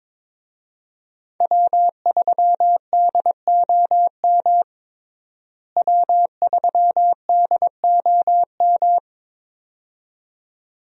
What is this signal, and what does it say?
Morse code, heard as a keyed tone, about 21 wpm, 700 Hz. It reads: W3DOM W3DOM
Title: USA Call Signs - 5 Characters - Repeated 2x - 22wpm